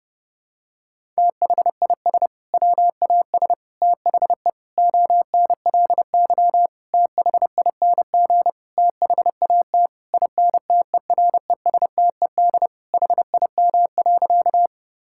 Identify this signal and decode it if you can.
Morse code, heard as a keyed tone, about 30 wpm, 700 Hz. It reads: THIS WAS THE ONLY THING THAT INTERESTED HIM.
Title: Sentences from Top 1000 Words - Review Part 6 - 30wpm